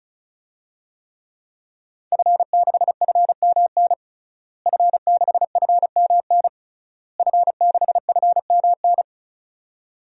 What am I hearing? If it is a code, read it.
Morse code, heard as a keyed tone, about 36 wpm, 700 Hz. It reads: F6FMN F6FMN F6FMN
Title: DXCC-100 Call Signs - 5 Characters - Repeated 3x - 35wpm